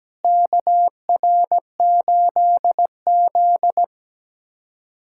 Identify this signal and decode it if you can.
Morse code, heard as a keyed tone, about 17 wpm, 700 Hz. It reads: KR8Z
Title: USA Callsign DE Callsign - 4 Characters - 17wpm